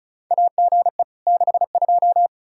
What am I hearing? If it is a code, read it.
Morse code, heard as a keyed tone, about 36 wpm, 700 Hz. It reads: AGE 62
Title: Practice QSO - Exchange 3 - More Info - 35wpm